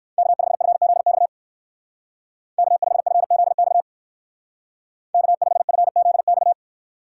Morse code, heard as a keyed tone, about 56 wpm, 700 Hz. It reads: K5FZX K5FZX K5FZX
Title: Speed Racing - USA 1x3 Call Signs 45wpm